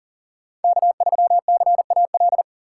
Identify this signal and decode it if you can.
Morse code, heard as a keyed tone, about 40 wpm, 700 Hz. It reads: K3CAL
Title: USA Call Signs - 5 Characters - 6x Character Spacing - 40wpm